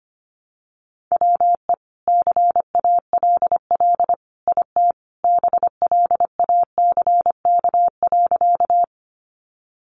Morse code, heard as a keyed tone, about 25 wpm, 700 Hz. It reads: WE CALL IT BLACK.
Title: Sentences from Top 400 Words 25wpm